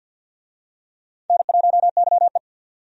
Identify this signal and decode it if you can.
Morse code, heard as a keyed tone, about 49 wpm, 700 Hz. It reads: N1YE
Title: USA Call Signs - 4 Characters - 50wpm